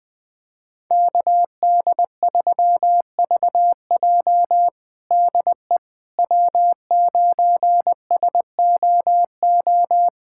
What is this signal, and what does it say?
Morse code, heard as a keyed tone, about 20 wpm, 700 Hz. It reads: KD3VJ DE W9SOO
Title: USA Callsign DE Callsign - 5 Characters - 20wpm